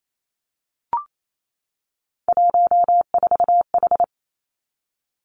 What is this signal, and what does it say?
Morse code, heard as a keyed tone, about 28 wpm, 700 Hz. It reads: 14H
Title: Three Challenging Letter-Number Combinations 28wpm